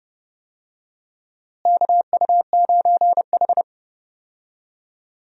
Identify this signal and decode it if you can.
Morse code, heard as a keyed tone, about 30 wpm, 700 Hz. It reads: KU9H
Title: USA Call Signs - 4 Characters - Repeated 2x - 30wpm